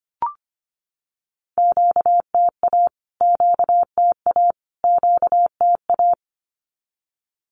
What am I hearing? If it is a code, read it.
Morse code, heard as a keyed tone, about 25 wpm, 700 Hz. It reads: QTA QTA QTA
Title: Q-Codes Repeated 3x 25wpm